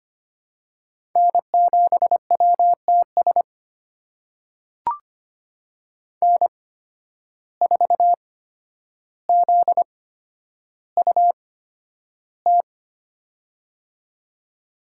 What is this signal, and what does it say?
Morse code, heard as a keyed tone, about 25 wpm, spaced as for 11 wpm, 700 Hz. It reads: N7WTS N4ZUT
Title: USA Call Signs - 5 Characters - 8x Character Spacing - 25wpm